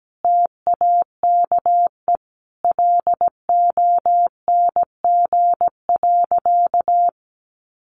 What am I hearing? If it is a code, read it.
Morse code, heard as a keyed tone, about 17 wpm, 700 Hz. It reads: TAKE LONG.
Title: Sentences from Top 600 Words 17wpm